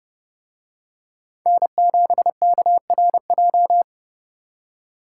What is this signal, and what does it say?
Morse code, heard as a keyed tone, about 29 wpm, 700 Hz. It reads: N7KRJ